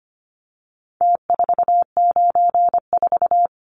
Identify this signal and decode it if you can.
Morse code, heard as a keyed tone, about 25 wpm, 700 Hz. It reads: T494
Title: Numbers - Contest Style - 4-Digit 25wpm